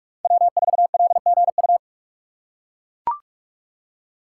Morse code, heard as a keyed tone, about 44 wpm, 700 Hz. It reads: W4LGU
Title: Code Groups - 5 Letter:Numbers - 45wpm